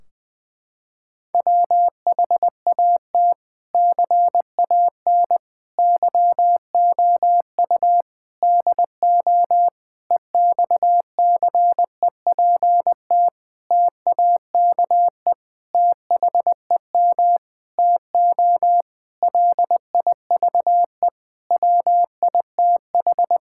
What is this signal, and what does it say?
Morse code, heard as a keyed tone, about 19 wpm, 700 Hz. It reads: WHAT CAN YOU DO EXCEPT TAKE THEM TO LIVE WITH
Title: Sentences from Top 1600 Words 20wpm